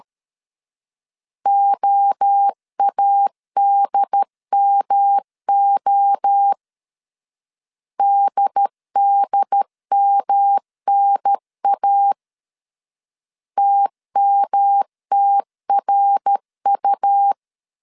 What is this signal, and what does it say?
Morse code, heard as a keyed tone, about 13 wpm, 800 Hz. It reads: OADMO DDMNA TMTRU
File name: CW / DET / 1 3 1500C S C 12wpm 2750